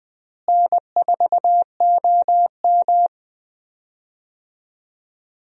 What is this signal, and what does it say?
Morse code, heard as a keyed tone, about 20 wpm, 700 Hz. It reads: N4OM
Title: USA Call Signs - 4 Characters - Repeated 2x - 20wpm